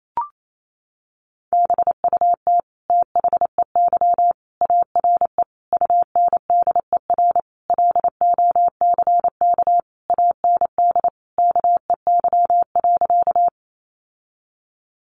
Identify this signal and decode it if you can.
Morse code, heard as a keyed tone, about 28 wpm, 700 Hz. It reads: BUT THEY ARE UNDER LOCK AND KEY.
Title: Sentences from Top 1300 Words 28wpm